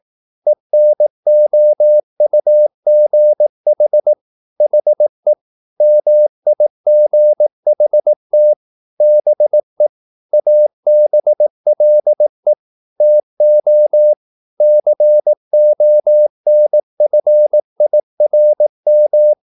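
Morse code, heard as a keyed tone, about 18 wpm, 600 Hz. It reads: ENOUGH HE MIGHT BE ABLE TO CONFIRM
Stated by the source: Morse practice at 18 wpm, English text, from CW med Gnister.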